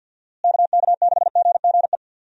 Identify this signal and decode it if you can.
Morse code, heard as a keyed tone, about 51 wpm, 700 Hz. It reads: KK6GGE
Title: USA Call Signs - 6 Characters - Repeated 2x - 50wpm